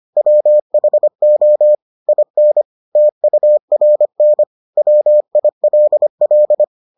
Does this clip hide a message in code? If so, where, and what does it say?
Yes: Morse code from 0.2 to 6.6 s, heard as a keyed tone, about 25 wpm, 600 Hz. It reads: WHO IN TURN WILL